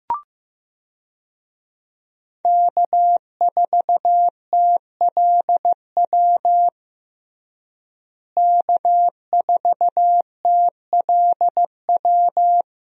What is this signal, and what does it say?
Morse code, heard as a keyed tone, about 15 wpm, 700 Hz. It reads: K4TLW K4TLW
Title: USA Call Signs - 5 Characters - Repeated 2x - 15wpm